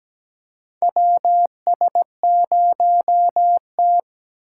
Morse code, heard as a keyed tone, about 17 wpm, 700 Hz. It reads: WS0T
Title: USA Call Signs - 2x Character Spacing - 17wpm